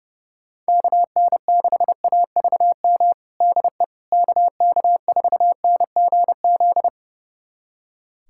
Morse code, heard as a keyed tone, about 29 wpm, 700 Hz. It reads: KN6AVM DE KK4NGZ
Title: USA Callsign DE Callsign 30wpm